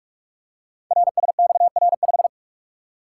Morse code, heard as a keyed tone, about 44 wpm, 700 Hz. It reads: RSXR5
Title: Code Groups - 5 Letter:Numbers - 45wpm